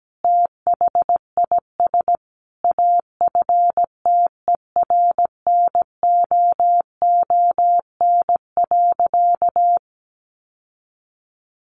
Morse code, heard as a keyed tone, about 17 wpm, 700 Hz. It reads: THIS AFTERNOON.
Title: Sentences from Top 1500 Words 17wpm